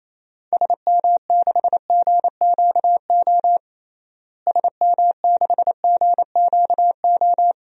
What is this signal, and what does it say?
Morse code, heard as a keyed tone, about 28 wpm, 700 Hz. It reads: SM6GQO SM6GQO
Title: DXCC-100 Call Signs - 6 Characters - Repeated 2x - 28wpm